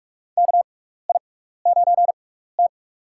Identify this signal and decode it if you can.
Morse code, heard as a keyed tone, about 43 wpm, spaced as for 15 wpm, 700 Hz. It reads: KI9T
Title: USA Call Signs - 4 Characters - 6x Character Spacing - 45wpm